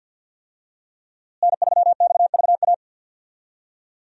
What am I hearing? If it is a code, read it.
Morse code, heard as a keyed tone, about 52 wpm, 700 Hz. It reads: N3XVA
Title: USA Call Signs - 5 Characters - Repeated 3x - 50wpm